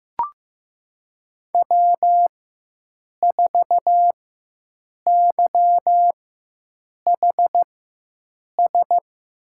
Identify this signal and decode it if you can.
Morse code, heard as a keyed tone, about 15 wpm, spaced as for 7 wpm, 700 Hz. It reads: W4YHS
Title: USA Call Signs - 5 Characters - 4x Character Spacing - 15wpm